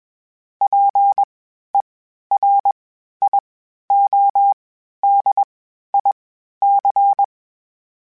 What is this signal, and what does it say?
Morse code, heard as a keyed tone, about 21 wpm, spaced as for 12 wpm, 800 Hz. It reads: PERIODIC
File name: (spaced) / FromSMs spaced mp3s / Antennas sp@25